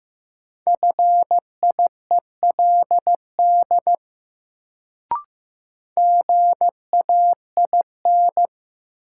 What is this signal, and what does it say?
Morse code, heard as a keyed tone, about 15 wpm, 700 Hz. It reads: FIELD GAIN
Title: Sentences from Top 1000 Words - New Word Drill - 15wpm